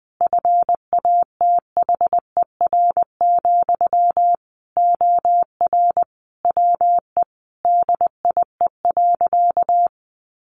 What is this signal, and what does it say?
Morse code, heard as a keyed tone, about 20 wpm, 700 Hz. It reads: FATHER, OR WE DIE.
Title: Sentences from Top 1300 Words 20wpm